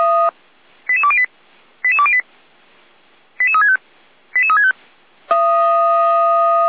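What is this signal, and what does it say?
Five-tone selective call (ZVEI), heard as quick runs of five tones, tones of about 70 ms. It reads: 88208 88208 88365 88365
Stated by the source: ZVEI five-tone selective call received by SDR.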